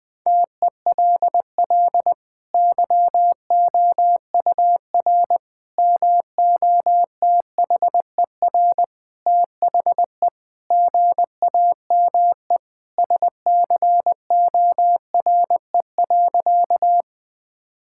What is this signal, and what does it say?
Morse code, heard as a keyed tone, about 20 wpm, 700 Hz. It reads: TELL YOUR MOTHER THE GAME SCORE.
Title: Sentences from Top 1600 Words 20wpm